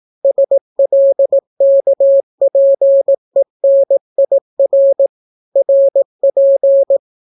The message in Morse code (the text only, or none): SLKPENIR RP